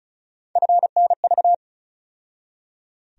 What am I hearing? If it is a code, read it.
Morse code, heard as a keyed tone, about 36 wpm, 700 Hz. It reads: FNV
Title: USA Call Sign Postfixes - 3 Letters 35wpm